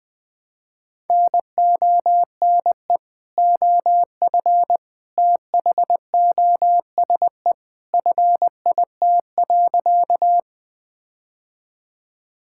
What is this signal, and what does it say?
Morse code, heard as a keyed tone, about 20 wpm, 700 Hz. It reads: NONE OF THOSE FIT.
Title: Sentences with 2-Word Phrases - 20wpm